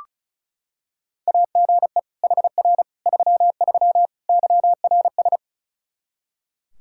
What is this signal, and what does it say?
Morse code, heard as a keyed tone, about 36 wpm, 700 Hz. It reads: AGE HR 33 YRS
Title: Practice QSO - Exchange 3 - More Info - 35wpm